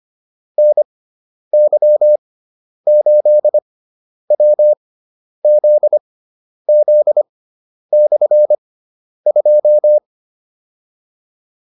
Morse code, heard as a keyed tone, about 25 wpm, spaced as for 10 wpm, 600 Hz. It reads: NY8WZZ/2
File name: QSO301 10